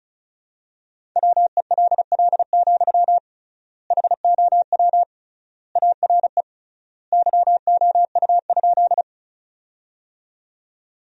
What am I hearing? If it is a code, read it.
Morse code, heard as a keyed tone, about 34 wpm, 700 Hz. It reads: WELL, HOW ARE YOU?
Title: 3x Word-Spacing - Sentences from Top 200 Words - 35wpm